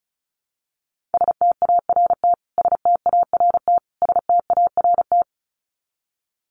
Morse code, heard as a keyed tone, about 34 wpm, 700 Hz. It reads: START START START
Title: Top 500 Words Repeated 35wpm